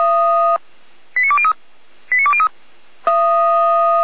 Five-tone selective call (ZVEI), heard as quick runs of five tones, tones of about 70 ms. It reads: 89203 89203